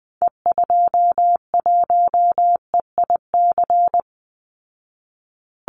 Morse code, heard as a keyed tone, about 20 wpm, 700 Hz. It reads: E21EIC
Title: Call Signs Repeated 3x - 20wpm